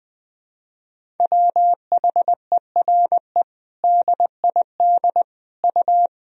Morse code, heard as a keyed tone, about 21 wpm, 700 Hz. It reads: WHERE DID U
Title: Sentences from Top 200 Words and Abbreviations 20wpm